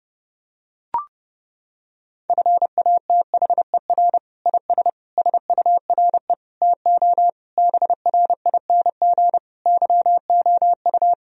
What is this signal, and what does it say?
Morse code, heard as a keyed tone, about 32 wpm, 700 Hz. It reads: FATHER IS SURE TO BRING YOU
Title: Sentences from Top 1000 Words - Review Part 3 - 30wpm